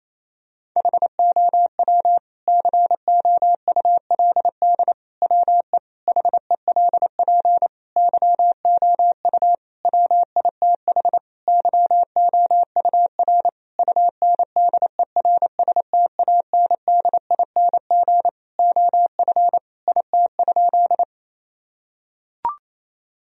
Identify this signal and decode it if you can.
Morse code, heard as a keyed tone, about 28 wpm, 700 Hz. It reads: HOW COULD WE HELP YOU WITH YOUR UNDERSTANDING OF IT?
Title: Sentences from Top 1000 Words - Review Part 3 - 28wpm